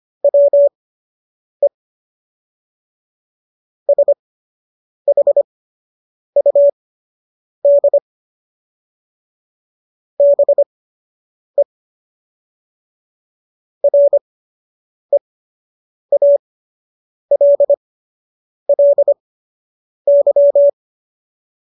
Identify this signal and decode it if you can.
Morse code, heard as a keyed tone, about 25 wpm, spaced as for 8 wpm, 570 Hz. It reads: WE SHUD BE REALLY
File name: qso006 08